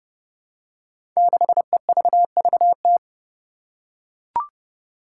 Morse code, heard as a keyed tone, about 30 wpm, 700 Hz. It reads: 6EVVT
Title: Five Challenging Letter-Number Combinations 30wpm